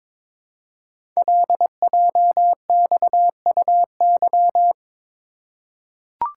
Morse code, heard as a keyed tone, about 22 wpm, 700 Hz. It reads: LJXUY
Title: Code Groups - 5 Letter:Numbers - 22wpm